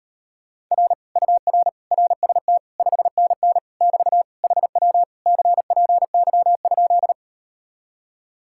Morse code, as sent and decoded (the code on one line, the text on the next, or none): .-. / ..- .-. / .-. ... - / ..... -. -. / -...- / .... .-- / -.-. .--. -.-- ..--..
R UR RST 5NN <BT> HW CPY?